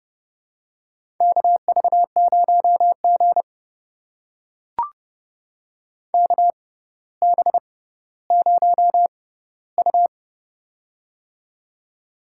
Morse code, heard as a keyed tone, about 30 wpm, spaced as for 15 wpm, 700 Hz. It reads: KV0G KB0U
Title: USA Call Signs - 4 Characters - 6x Character Spacing - 30wpm